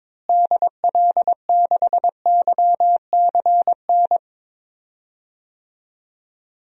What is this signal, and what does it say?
Morse code, heard as a keyed tone, about 22 wpm, 700 Hz. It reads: DL6YCN
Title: DXCC-100 Call Signs - 6 Characters - Repeated 3x - 22wpm